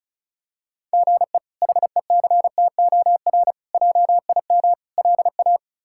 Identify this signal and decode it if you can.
Morse code, heard as a keyed tone, about 34 wpm, 700 Hz. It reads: GE HECTOR JIM LA
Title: K1USN SST QSO Practice - 35wpm